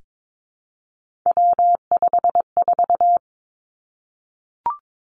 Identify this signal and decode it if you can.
Morse code, heard as a keyed tone, about 22 wpm, 700 Hz. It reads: W54
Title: Three Challenging Letter-Number Combinations 22wpm